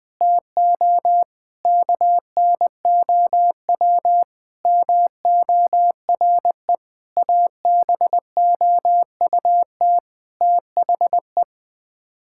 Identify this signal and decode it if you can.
Morse code, heard as a keyed tone, about 20 wpm, 700 Hz. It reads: TO KNOW MORE ABOUT THE
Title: Sets of 8 Words - 20wpm